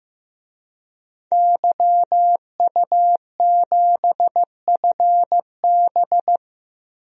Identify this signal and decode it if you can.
Morse code, heard as a keyed tone, about 15 wpm, 700 Hz. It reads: YU7FB
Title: DXCC-100 Call Signs - 5 Characters - 15wpm